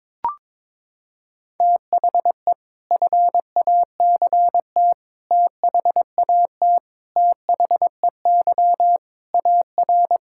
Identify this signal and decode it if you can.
Morse code, heard as a keyed tone, about 22 wpm, 700 Hz. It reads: THE FACT THAT THEY AR
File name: practice / 22wpm / Sentences with 4-Word Phrases - 22wpm